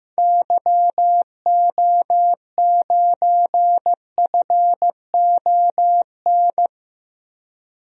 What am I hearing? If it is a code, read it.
Morse code, heard as a keyed tone, about 16 wpm, 700 Hz. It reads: YO9FON
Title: DXCC-100 Call Signs - 6 Characters - 15wpm